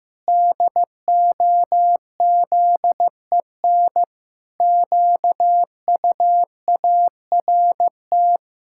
Morse code, heard as a keyed tone, about 15 wpm, 700 Hz. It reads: DOZEN QUART
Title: Pangram Challenge v2 - 15wpm